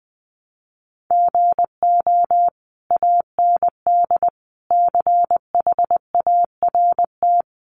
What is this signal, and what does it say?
Morse code, heard as a keyed tone, about 20 wpm, 700 Hz. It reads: GO AND CHART
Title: Sentences from Top 1300 Words 20wpm